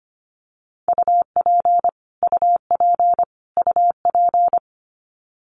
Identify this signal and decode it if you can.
Morse code, heard as a keyed tone, about 25 wpm, 700 Hz. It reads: UP UP UP
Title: Top 500 Words Repeated 25wpm